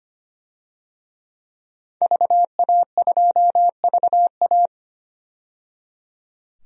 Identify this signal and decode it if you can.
Morse code, heard as a keyed tone, about 25 wpm, 700 Hz. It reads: VA2VA
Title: DXCC-100 Call Signs - 5 Characters - 25wpm